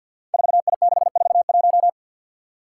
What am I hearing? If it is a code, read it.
Morse code, heard as a keyed tone, about 51 wpm, 700 Hz. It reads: 4I641
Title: Five Challenging Letter-Number Combinations 50wpm